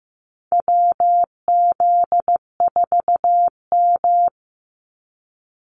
Morse code, heard as a keyed tone, about 15 wpm, 700 Hz. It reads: WZ4M
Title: NA SOTA Activator Callsigns 15wpm